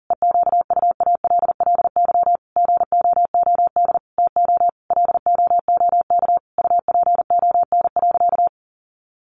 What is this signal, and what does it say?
Morse code, heard as a keyed tone, about 40 wpm, 700 Hz. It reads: EQUALLY GOOD TO LOOK UPON.
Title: Sentences from Top 1400 Words 40wpm